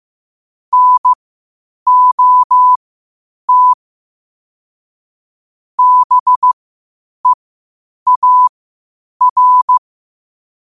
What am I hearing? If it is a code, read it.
Morse code, heard as a keyed tone, about 15 wpm, spaced as for 8 wpm, 1000 Hz. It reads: NOT BEAR
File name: qotdfiles / 8wpm